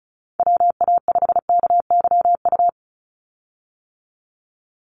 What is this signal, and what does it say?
Morse code, heard as a keyed tone, about 36 wpm, 700 Hz. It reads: WA5KYU
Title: USA Call Signs 35wpm